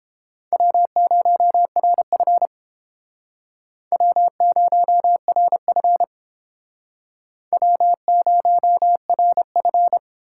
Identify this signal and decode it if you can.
Morse code, heard as a keyed tone, about 30 wpm, 700 Hz. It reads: W0RF W0RF W0RF